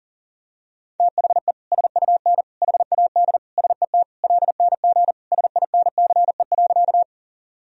Morse code, heard as a keyed tone, about 40 wpm, 700 Hz. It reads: THE SUN HAD SET LNG SINCE.
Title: Sentences from Top 300 Words and Abbreviations 40wpm